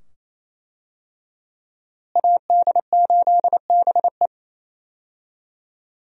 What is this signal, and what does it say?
Morse code, heard as a keyed tone, about 28 wpm, 700 Hz. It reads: AD8BE
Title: USA Call Signs - 5 Characters - Repeated 2x - 28wpm